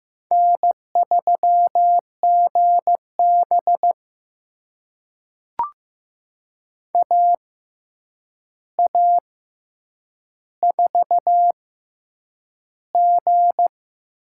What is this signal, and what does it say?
Morse code, heard as a keyed tone, about 15 wpm, spaced as for 8 wpm, 700 Hz. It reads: N3GB AA4G
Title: USA Call Signs - 6x Character Spacing - 15wpm